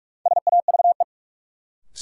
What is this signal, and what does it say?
Morse code, heard as a keyed tone, about 42 wpm, 700 Hz. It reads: SAVE
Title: Sentences from Top 1100 Words - New Word Drill Alt - 45wpm